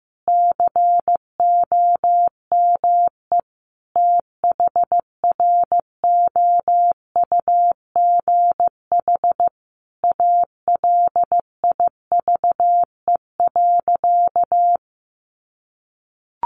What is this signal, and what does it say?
Morse code, heard as a keyed tone, about 15 wpm, 700 Hz. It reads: COME THROUGH ALIVE.
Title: A Sound Of Thunder 15wpm